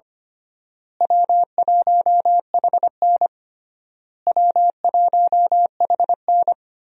Morse code, heard as a keyed tone, about 25 wpm, 700 Hz. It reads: W1HN W1HN
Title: USA Call Signs - 4 Characters - Repeated 3x - 25wpm